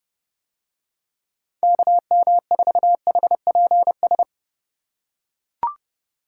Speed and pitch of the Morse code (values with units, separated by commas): 29 wpm, 700 Hz